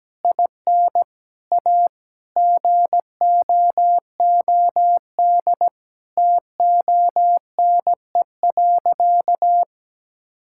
Morse code, heard as a keyed tone, about 17 wpm, 700 Hz. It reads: IN A GOOD TONE.
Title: Sentences from Top 1600 Words 17wpm